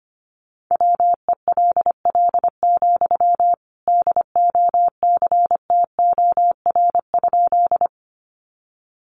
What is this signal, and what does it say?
Morse code, heard as a keyed tone, about 25 wpm, 700 Hz. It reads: WELL, DOCTOR?
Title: Sentences from Top 1500 Words 25wpm